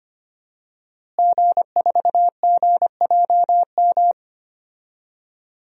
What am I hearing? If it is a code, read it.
Morse code, heard as a keyed tone, about 25 wpm, 700 Hz. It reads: G4GJM
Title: DXCC-100 Call Signs - 5 Characters - Repeated 2x - 25wpm